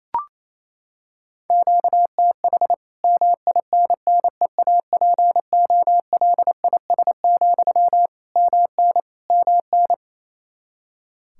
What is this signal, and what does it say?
Morse code, heard as a keyed tone, about 27 wpm, 700 Hz. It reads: QTH MINNEAPOLIS, MN MN
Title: Practice QSO - Exchange 1 - Beginning - 28wpm